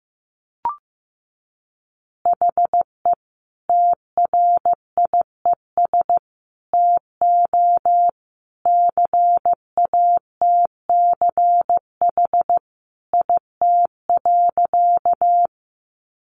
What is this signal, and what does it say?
Morse code, heard as a keyed tone, about 15 wpm, 700 Hz. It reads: HE TRIES TO CATCH IT.
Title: Additional Easy Phrases 15wpm